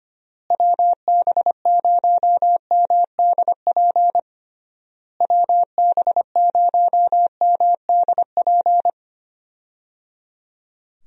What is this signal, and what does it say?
Morse code, heard as a keyed tone, about 25 wpm, 700 Hz. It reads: WB0MDP WB0MDP
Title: USA Call Signs - 6 Characters - Repeated 3x - 25wpm